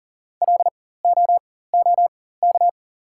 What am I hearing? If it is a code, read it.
Morse code, heard as a keyed tone, about 40 wpm, spaced as for 19 wpm, 700 Hz. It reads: LOOK
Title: 4x Character Spacing - Top 300 Words - 40wpm